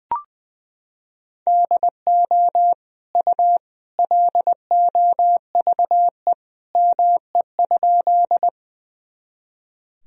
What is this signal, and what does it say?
Morse code, heard as a keyed tone, about 21 wpm, 700 Hz. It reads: DO U LOVE ME?
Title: Sentences from Top 400 Words and Abbreviations 20wpm